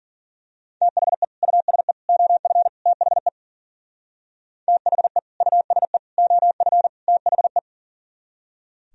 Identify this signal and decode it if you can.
Morse code, heard as a keyed tone, about 42 wpm, 700 Hz. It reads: THE USE OF THE THE USE OF THE
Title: Speed-Racing - Sets of 4 Words 40wpm